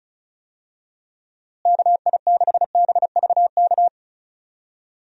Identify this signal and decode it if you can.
Morse code, heard as a keyed tone, about 34 wpm, 700 Hz. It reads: KI6BVK
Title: USA Call Signs - 6 Characters - Repeated 3x - 35wpm